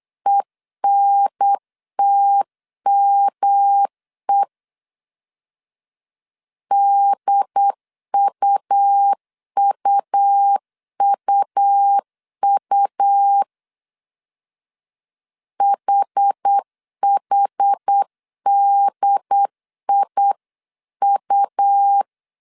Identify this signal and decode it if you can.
Morse code, heard as a keyed tone, about 8 wpm, 800 Hz. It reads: ENTME DUUUU HHDIU